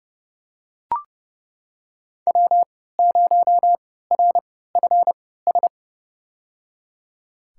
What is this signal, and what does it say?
Morse code, heard as a keyed tone, about 31 wpm, spaced as for 17 wpm, 700 Hz. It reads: W0RFS
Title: USA Call Signs - 5 Characters - 3x Character Spacing - 30wpm